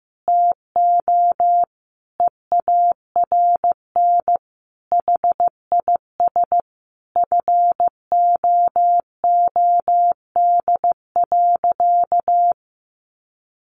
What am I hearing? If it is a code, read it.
Morse code, heard as a keyed tone, about 15 wpm, 700 Hz. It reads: TO EARN HIS FOOD.
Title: Sentences from Top 1500 Words 15wpm